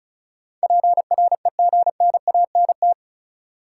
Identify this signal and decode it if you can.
Morse code, heard as a keyed tone, about 34 wpm, 700 Hz. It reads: PREGNANT
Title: Sentences from Top 1000 Words - Review Part 6 - New Word Drill - 35wpm